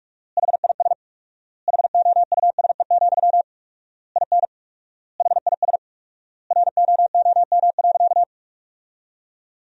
Morse code, heard as a keyed tone, about 42 wpm, 700 Hz. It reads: HIS HOUSE, IN HIS ROOM.
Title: 4x Word-Spacing - Sentences from Top 300 Words - 45wpm